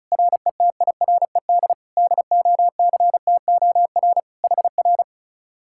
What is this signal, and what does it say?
Morse code, heard as a keyed tone, about 38 wpm, 700 Hz. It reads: RETIRED DOCTOR HR